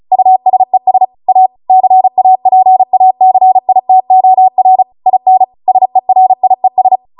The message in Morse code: USES A CAPACITOR IN SERIES